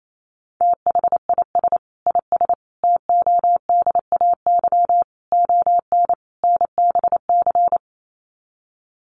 Morse code, heard as a keyed tone, about 28 wpm, 700 Hz. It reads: THIS IS TODAY ON NBC
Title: Sets of 5 Words Encore 28wpm